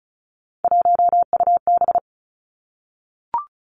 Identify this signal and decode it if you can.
Morse code, heard as a keyed tone, about 36 wpm, 700 Hz. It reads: 1UB
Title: Three Challenging Letter-Number Combinations 35wpm